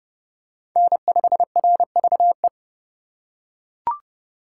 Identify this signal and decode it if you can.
Morse code, heard as a keyed tone, about 31 wpm, 700 Hz. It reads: N5RVE